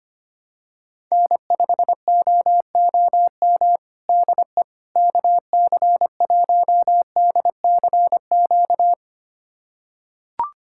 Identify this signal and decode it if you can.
Morse code, heard as a keyed tone, about 25 wpm, 700 Hz. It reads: N5OOM DE KC1DCQ
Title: USA Callsign DE Callsign 25wpm